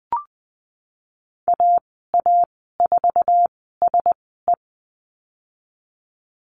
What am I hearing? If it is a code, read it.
Morse code, heard as a keyed tone, about 20 wpm, spaced as for 15 wpm, 700 Hz. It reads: AA4SE
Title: 055 Lesson 10g - Letter r - Call Signs 20wpm